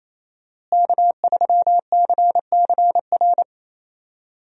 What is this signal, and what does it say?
Morse code, heard as a keyed tone, about 28 wpm, 700 Hz. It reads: K3CCR
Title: USA Call Signs - 5 Characters - Repeated 3x - 28wpm